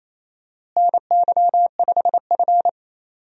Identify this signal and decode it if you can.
Morse code, heard as a keyed tone, about 28 wpm, 700 Hz. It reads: NY5F
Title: USA Call Signs - 4 Characters - Repeated 3x - 28wpm